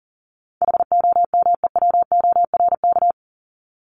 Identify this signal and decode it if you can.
Morse code, heard as a keyed tone, about 40 wpm, 700 Hz. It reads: HOMEWORK
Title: Top 1501-1750 Words 40wpm